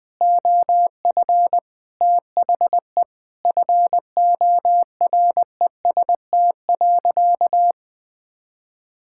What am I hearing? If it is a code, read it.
Morse code, heard as a keyed tone, about 20 wpm, 700 Hz. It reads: OF THE FOREST.
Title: Sentences with 5-Word Phrases - 20wpm